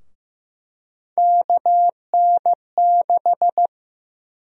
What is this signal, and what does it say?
Morse code, heard as a keyed tone, about 15 wpm, 700 Hz. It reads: KN6
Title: USA Call Sign Prefixes - 15wpm